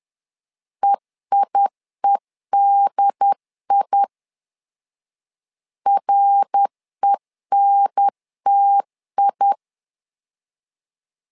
Morse code, heard as a keyed tone, about 10 wpm, 800 Hz. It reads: EIEDI RENTI
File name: LEZIONI / 3 MWRO / 3 MWR 10wpm 1952